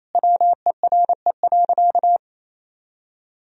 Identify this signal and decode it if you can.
Morse code, heard as a keyed tone, about 28 wpm, 700 Hz. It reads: WERE.
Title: Sentences from Top 1000 Words - Review Part 4 - 28wpm